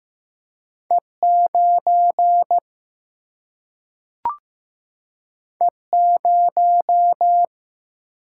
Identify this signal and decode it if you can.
Morse code, heard as a keyed tone, about 15 wpm, 700 Hz. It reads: E 9 E 0
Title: Single-Letter Single-Number 15wpm